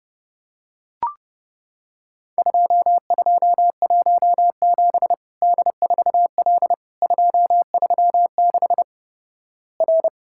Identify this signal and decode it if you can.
Morse code, heard as a keyed tone, about 30 wpm, 700 Hz. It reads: 2217 D4L 236 R
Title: Worked All Europe DX Contest - QTC Exchange - 30wpm